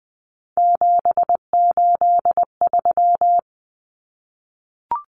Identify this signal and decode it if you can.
Morse code, heard as a keyed tone, about 20 wpm, 700 Hz. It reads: 783
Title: Common Cut Numbers - 3-Digit 20wpm